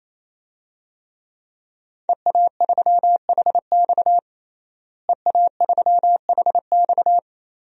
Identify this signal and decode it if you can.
Morse code, heard as a keyed tone, about 28 wpm, 700 Hz. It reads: EA3HX EA3HX
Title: DXCC-100 Call Signs - 5 Characters - Repeated 3x - 28wpm